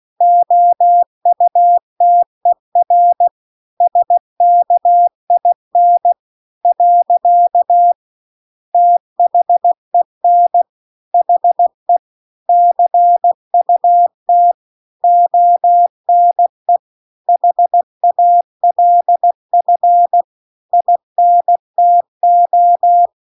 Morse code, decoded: OUTER SKIN . THEN HE CUT ONE HALF INTO

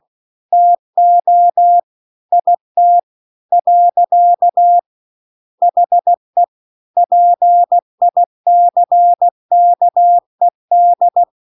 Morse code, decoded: TO IT . HE PICKED

